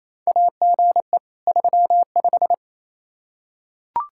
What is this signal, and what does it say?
Morse code, heard as a keyed tone, about 28 wpm, 700 Hz. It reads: AGE 35
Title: Practice QSO - Exchange 3 - More Info - 28wpm